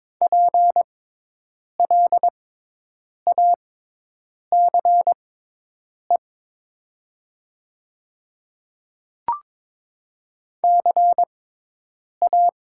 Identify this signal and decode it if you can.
Morse code, heard as a keyed tone, about 22 wpm, spaced as for 8 wpm, 700 Hz. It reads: PLACE CA
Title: 6x Character Spacing - Top 300 Words - 22wpm